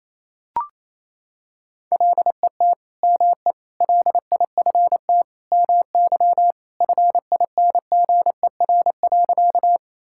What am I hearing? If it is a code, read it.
Morse code, heard as a keyed tone, about 28 wpm, 700 Hz. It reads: LET ME LIFT MY FINGER.
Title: Sentences from Top 1600 Words 28wpm